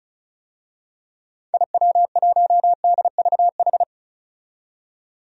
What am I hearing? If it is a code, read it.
Morse code, heard as a keyed tone, about 34 wpm, 700 Hz. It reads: IW1DVH
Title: DXCC-100 Call Signs - 6 Characters - 35wpm